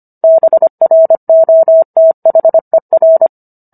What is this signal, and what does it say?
Morse code, heard as a keyed tone, about 25 wpm, 650 Hz. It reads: BROTHER